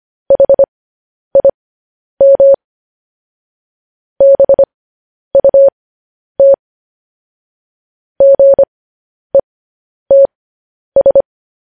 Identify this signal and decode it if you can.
Morse code, heard as a keyed tone, about 25 wpm, spaced as for 10 wpm, 550 Hz. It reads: HIM BUT GETS
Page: news-headlines-in-morse-code-at-10-wpm